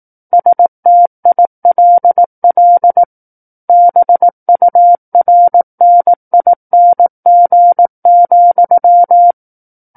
Morse code, heard as a keyed tone, about 18 wpm, 700 Hz. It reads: STILL BURNING,
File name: 18 WPM